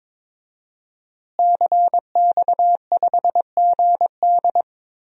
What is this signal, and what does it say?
Morse code, heard as a keyed tone, about 22 wpm, 700 Hz. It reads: CX5GD
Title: DXCC-100 Call Signs - 5 Characters - Repeated 2x - 22wpm